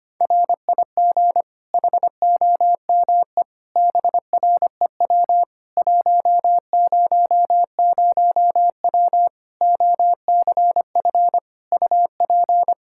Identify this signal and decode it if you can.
Morse code, heard as a keyed tone, about 25 wpm, 700 Hz. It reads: RIG HOME BREW 100W OCF UP
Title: K1USN SST QSO Practice 25wpm